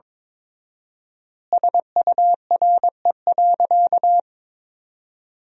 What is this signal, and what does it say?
Morse code, heard as a keyed tone, about 23 wpm, 700 Hz. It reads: SURE.
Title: 4x Word-Spacing - Sentences from Top 500 Words - 22wpm